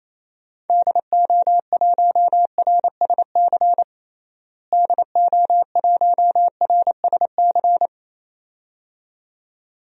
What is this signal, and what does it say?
Morse code, heard as a keyed tone, about 28 wpm, 700 Hz. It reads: DO1RSC DO1RSC
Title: DXCC-100 Call Signs - 6 Characters - Repeated 2x - 28wpm